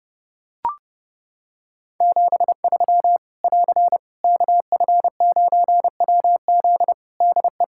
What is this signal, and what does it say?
Morse code, heard as a keyed tone, about 30 wpm, 700 Hz. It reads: 73 <AR> KF9WZ DE
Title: Practice QSO - Exchange 4 - Ending - 30wpm